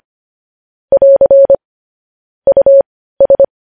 Morse code, heard as a keyed tone, about 25 wpm, spaced as for 15 wpm, 550 Hz. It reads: <AR> US
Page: news headlines in morse code at 15 wpm